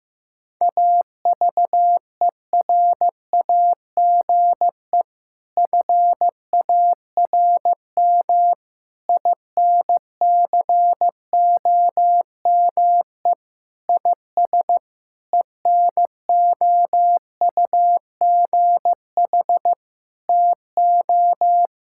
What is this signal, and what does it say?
Morse code, heard as a keyed tone, about 15 wpm, 700 Hz. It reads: AVERAGE FARM INCOME IS ENOUGH TO
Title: Sentences from Top 1000 Words - Review Part 6 - 15wpm